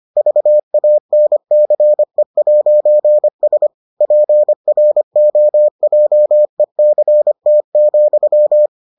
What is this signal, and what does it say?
Morse code, heard as a keyed tone, about 25 wpm, 600 Hz. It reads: VANCE'S PROJECT,